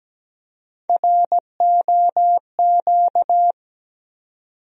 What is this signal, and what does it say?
Morse code, heard as a keyed tone, about 17 wpm, 700 Hz. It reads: ROQ
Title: USA Call Sign Postfixes - 3 Letters 17wpm